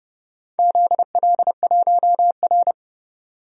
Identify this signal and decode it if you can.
Morse code, heard as a keyed tone, about 30 wpm, 700 Hz. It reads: ZL1R
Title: DXCC-100 Call Signs - 4 Characters - 30wpm